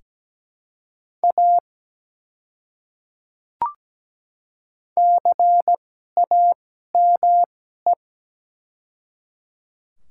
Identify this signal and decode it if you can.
Morse code, heard as a keyed tone, about 17 wpm, spaced as for 12 wpm, 700 Hz. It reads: A CAME
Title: 2x Character Spacing - Top 200 Words - 17wpm